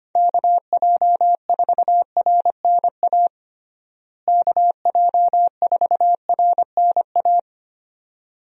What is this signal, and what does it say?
Morse code, heard as a keyed tone, about 25 wpm, 700 Hz. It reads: KJ4RNA KJ4RNA
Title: USA Call Signs - 6 Characters - Repeated 2x - 25wpm